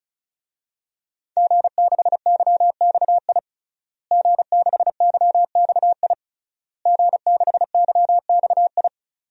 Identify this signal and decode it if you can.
Morse code, heard as a keyed tone, about 34 wpm, 700 Hz. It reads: G6YXI G6YXI G6YXI
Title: DXCC-100 Call Signs - 5 Characters - Repeated 3x - 35wpm